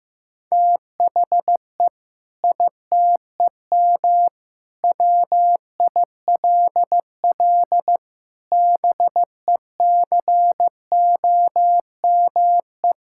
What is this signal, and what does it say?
Morse code, heard as a keyed tone, about 15 wpm, 700 Hz. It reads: THE ITEM WILL BECOME
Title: Sentences from Top 1000 Words - Review Part 6 - 15wpm